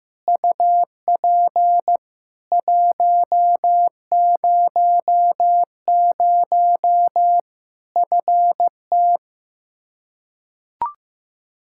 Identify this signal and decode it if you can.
Morse code, heard as a keyed tone, about 15 wpm, 700 Hz. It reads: UP 100 FT
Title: Practice QSO - Exchange 2 - Info - 15wpm